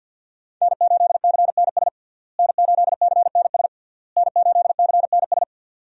Morse code, heard as a keyed tone, about 52 wpm, 700 Hz. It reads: N8CNS N8CNS N8CNS
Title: USA Call Signs - 5 Characters - Repeated 3x - 50wpm